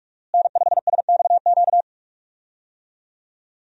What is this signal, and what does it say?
Morse code, heard as a keyed tone, about 46 wpm, 700 Hz. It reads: N5SXQ
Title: USA Call Signs - 5 Characters - Repeated 2x - 45wpm